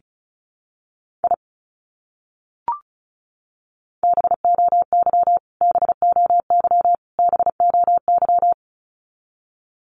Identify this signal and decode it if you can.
Morse code, heard as a keyed tone, about 34 wpm, 700 Hz. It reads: I BOY BOY BOY
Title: Top 200 Words Repeated 35wpm